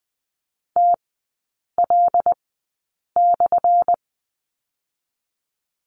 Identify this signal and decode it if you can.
Morse code, heard as a keyed tone, about 20 wpm, spaced as for 8 wpm, 700 Hz. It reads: TL/
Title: 218 Lesson 35e - Number 0 - Three Characters 20wpm